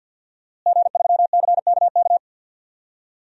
Morse code, heard as a keyed tone, about 46 wpm, 700 Hz. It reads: G3CKK
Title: DXCC-100 Call Signs - 5 Characters - Repeated 2x - 50wpm